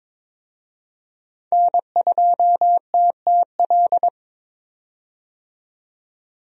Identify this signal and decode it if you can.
Morse code, heard as a keyed tone, about 22 wpm, 700 Hz. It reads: N2TTL
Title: USA Call Signs - 5 Characters - 22wpm